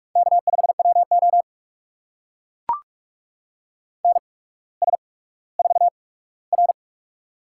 Morse code, heard as a keyed tone, about 44 wpm, spaced as for 19 wpm, 700 Hz. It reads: K5WO NS4R
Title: USA Call Signs - 4 Characters - 8x Character Spacing - 45wpm